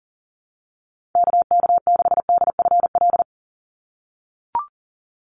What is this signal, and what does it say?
Morse code, heard as a keyed tone, about 40 wpm, 700 Hz. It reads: KK6DFL
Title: USA Call Signs 40wpm